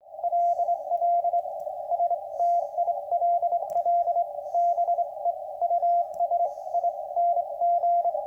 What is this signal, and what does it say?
Morse code, heard as a keyed tone, about 23 wpm, spaced as for 20 wpm, 690 Hz, with radio noise. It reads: LL STILL BE USING